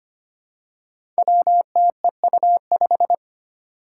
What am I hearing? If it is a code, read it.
Morse code, heard as a keyed tone, about 25 wpm, 700 Hz. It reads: WTEU5
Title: Five Challenging Letter-Number Combinations 25wpm